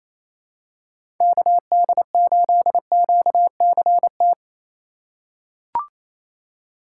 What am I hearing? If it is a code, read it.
Morse code, heard as a keyed tone, about 28 wpm, 700 Hz. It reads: KD8QCT
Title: USA Call Signs - 6 Characters - 4x Character Spacing - 28wpm